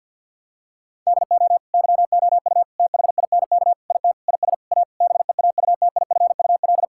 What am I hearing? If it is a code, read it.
Morse code, heard as a keyed tone, about 48 wpm, 700 Hz. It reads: DO YOU THINK IT IS A BEAUTIFUL
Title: Sentences from Top 1000 Words - Review Part 6 - 50wpm